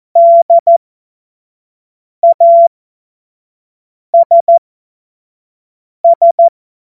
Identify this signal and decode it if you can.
Morse code, heard as a keyed tone, about 14 wpm, spaced as for 5 wpm, 680 Hz. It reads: DASS
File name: Morsekurs / Witze / Witz 11. April 2023 25 F70